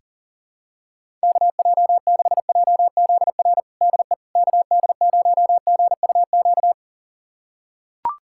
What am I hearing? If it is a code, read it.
Morse code, heard as a keyed tone, about 40 wpm, 700 Hz. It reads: KJ6JZR DE KD0GUQ
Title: USA Callsign DE Callsign 40wpm